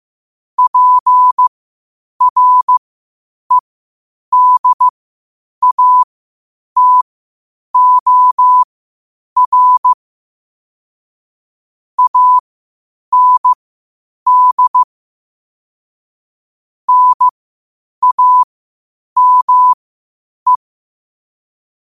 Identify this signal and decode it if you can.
Morse code, heard as a keyed tone, about 15 wpm, spaced as for 8 wpm, 1000 Hz. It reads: PREDATOR AND NAME
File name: qotdfiles / 8wpm